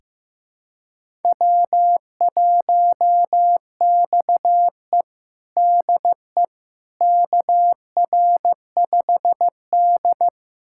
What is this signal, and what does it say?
Morse code, heard as a keyed tone, about 15 wpm, 700 Hz. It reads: W1XE DE KR5D
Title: USA Callsign DE Callsign - 4 Characters - 15wpm